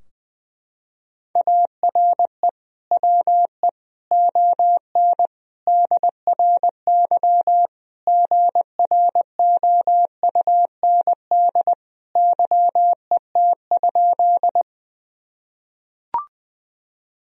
Morse code, heard as a keyed tone, about 20 wpm, 700 Hz. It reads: ARE WE ON DRY GROUND YET?
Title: Sentences from Top 1000 Words - Review Part 3 - 20wpm